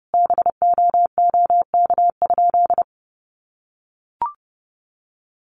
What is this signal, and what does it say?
Morse code, heard as a keyed tone, about 30 wpm, 700 Hz. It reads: BOOK?
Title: Sentences from Top 1300 Words 30wpm